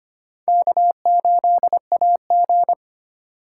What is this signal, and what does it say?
Morse code, heard as a keyed tone, about 31 wpm, 700 Hz. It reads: K8AG